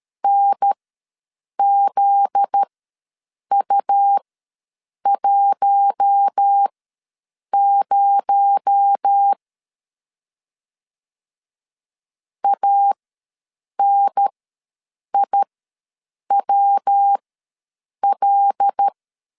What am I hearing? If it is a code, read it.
Morse code, heard as a keyed tone, about 13 wpm, spaced as for 7 wpm, 800 Hz. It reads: NZU10 ANIWL
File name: 7 GF9Z ws 12wpm